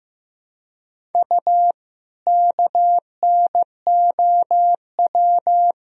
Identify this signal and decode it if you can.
Morse code, heard as a keyed tone, about 15 wpm, 700 Hz. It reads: U KNOW
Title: Sentences from Top 100 Words and Abbreviations 15wpm